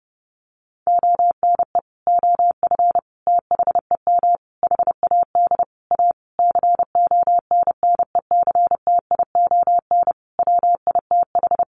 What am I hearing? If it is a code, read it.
Morse code, heard as a keyed tone, about 30 wpm, 700 Hz. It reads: ONE OF THEM HAD A CONNECTION WITH
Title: Sentences from Top 1300 Words 30wpm